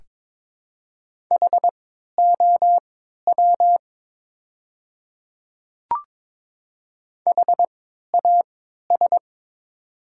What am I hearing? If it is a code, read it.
Morse code, heard as a keyed tone, about 23 wpm, spaced as for 13 wpm, 700 Hz. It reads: HOW HAS
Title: 3x Character Spacing - Top 100 Words - 22wpm